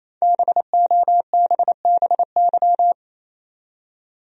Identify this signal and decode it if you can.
Morse code, heard as a keyed tone, about 28 wpm, 700 Hz. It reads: BOBBY
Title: USA Common Names - 28wpm